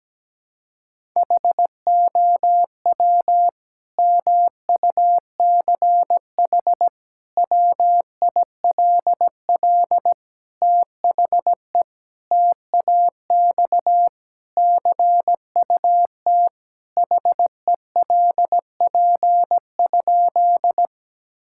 Morse code, heard as a keyed tone, about 17 wpm, 700 Hz. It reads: HOW MUCH WILL THE TAX CUT HELP?
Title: Sentences from Top 1000 Words - Review Part 5 - 17wpm